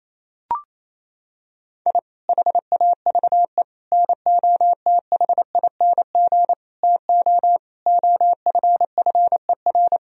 Morse code, heard as a keyed tone, about 28 wpm, 700 Hz. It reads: I HAVE NOTHING TO OFFER
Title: Sentences from Top 1000 Words - Review Part 4 - 28wpm